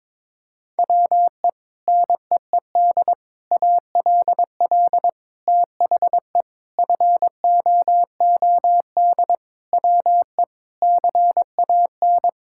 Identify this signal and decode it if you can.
Morse code, heard as a keyed tone, about 22 wpm, 700 Hz. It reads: WE NEED ALL THE FOOD WE CAN
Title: Sentences from Top 1000 Words - Review Part 5 - 22wpm